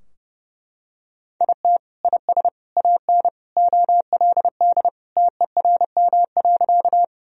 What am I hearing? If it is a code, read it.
Morse code, heard as a keyed tone, about 30 wpm, 700 Hz. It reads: IT IS AN OLD TERM.
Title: Sentences from Top 1000 Words - Review Part 4 - 30wpm